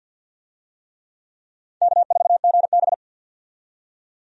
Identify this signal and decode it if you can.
Morse code, heard as a keyed tone, about 54 wpm, 700 Hz. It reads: K4GB
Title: USA Call Signs - 4 Characters - 50wpm